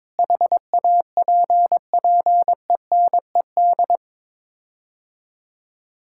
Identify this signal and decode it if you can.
Morse code, heard as a keyed tone, about 22 wpm, 700 Hz. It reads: HAPPENED
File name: practice / 22wpm / Sets of 8 Words - 22wpm